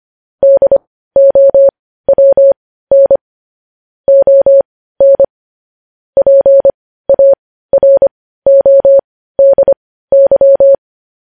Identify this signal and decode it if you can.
Morse code, heard as a keyed tone, about 25 wpm, spaced as for 15 wpm, 550 Hz. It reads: DOWN ON PARODY
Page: news-headlines-in-morse-code-at-15-wpm